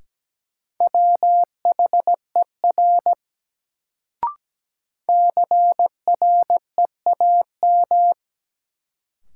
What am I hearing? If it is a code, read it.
Morse code, heard as a keyed tone, about 17 wpm, 700 Hz. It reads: WHER CREAM
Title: Sentences from Top 1500 Words - New Word Drill Alt - 17wpm